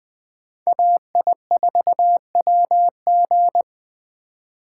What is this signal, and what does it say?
Morse code, heard as a keyed tone, about 20 wpm, 700 Hz. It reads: AI4WG
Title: Speed Racing - USA Call Signs 17wpm